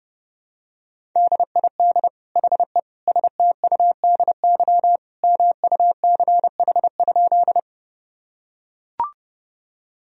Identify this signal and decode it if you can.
Morse code, heard as a keyed tone, about 30 wpm, 700 Hz. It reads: DID HE STUDY MUCH?
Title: Sentences from Top 1000 Words - Review Part 1 - 30wpm